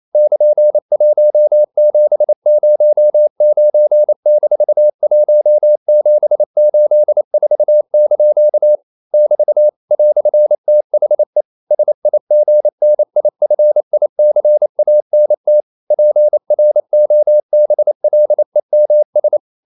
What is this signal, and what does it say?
Morse code, heard as a keyed tone, about 28 wpm, 600 Hz. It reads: <KN>1709-1784) <BT> "THE SIGNIFICANT PROBLEMS
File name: CW / Practice / EnglishQuotations0228